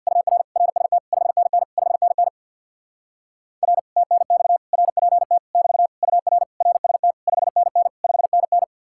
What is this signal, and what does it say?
Morse code, heard as a keyed tone, about 67 wpm, 700 Hz. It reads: UR RST 5NN 5NN R TNX RPT <BT> UR RST 5NN 5NN